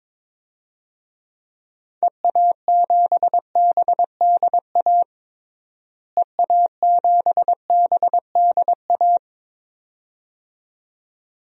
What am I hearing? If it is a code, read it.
Morse code, heard as a keyed tone, about 22 wpm, 700 Hz. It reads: EA7BDA EA7BDA
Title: DXCC-100 Call Signs - 6 Characters - Repeated 2x - 22wpm